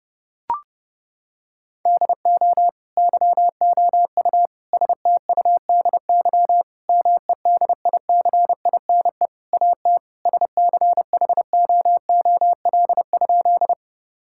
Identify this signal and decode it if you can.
Morse code, heard as a keyed tone, about 30 wpm, 700 Hz. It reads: DO YOU STUDY MEDICINE AT SCHOOL?
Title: Sentences from Top 1000 Words - Review Part 6 - 30wpm